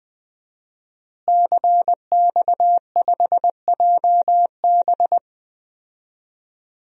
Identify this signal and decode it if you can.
Morse code, heard as a keyed tone, about 20 wpm, 700 Hz. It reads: CX5JB
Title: DXCC-100 Call Signs - 5 Characters - Repeated 3x - 20wpm